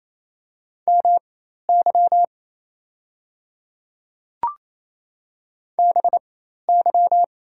Morse code, heard as a keyed tone, about 28 wpm, spaced as for 13 wpm, 690 Hz. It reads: MY BY
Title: 4x Character Spacing - Top 200 Words - 28wpm